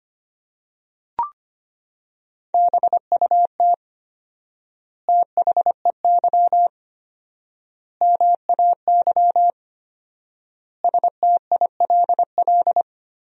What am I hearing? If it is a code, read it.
Morse code, heard as a keyed tone, about 25 wpm, 700 Hz. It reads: BUT THEY MAY STILL
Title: 4x Word-Spacing - Sentences from Top 300 Words - 25wpm